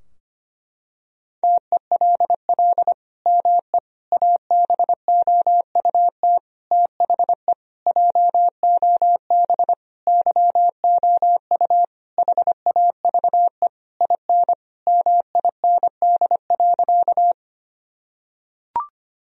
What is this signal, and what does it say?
Morse code, heard as a keyed tone, about 25 wpm, 700 Hz. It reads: TELL ME ABOUT THE JOB YOU HAVE IN MIND.
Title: Sentences from Top 1000 Words - Review Part 3 - 25wpm